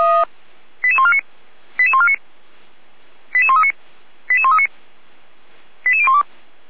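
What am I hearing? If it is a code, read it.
Five-tone selective call (ZVEI), heard as quick runs of five tones, tones of about 70 ms. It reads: 88149 88149 88139 88139 88012
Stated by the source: ZVEI five-tone call 88402 received by SDR, from the feuerwehr-orlamuende.org fire brigade archive.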